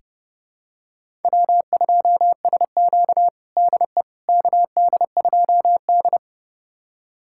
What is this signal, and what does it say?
Morse code, heard as a keyed tone, about 32 wpm, 700 Hz. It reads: W2SQ DE KD2D
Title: USA Callsign DE Callsign - 4 Characters - 30wpm